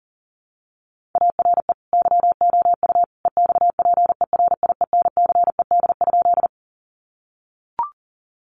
Morse code, heard as a keyed tone, about 40 wpm, 700 Hz. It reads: ARE YOU EXPERIENCED?
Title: Sentences from Top 1100 Words 40wpm